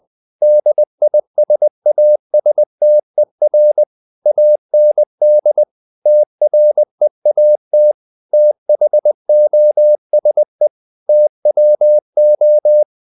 Morse code, heard as a keyed tone, about 20 wpm, 600 Hz. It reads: DISASTER AND TREAT THOSE TWO
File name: CW / Practice / If20